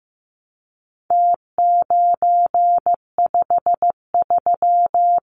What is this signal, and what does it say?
Morse code, heard as a keyed tone, about 15 wpm, 700 Hz. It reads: T953
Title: Numbers - Contest Style - 4-Digit 15wpm